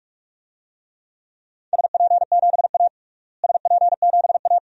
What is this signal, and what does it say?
Morse code, heard as a keyed tone, about 46 wpm, 700 Hz. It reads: SP7A SP7A
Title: DXCC-100 Call Signs - 4 Characters - Repeated 2x - 45wpm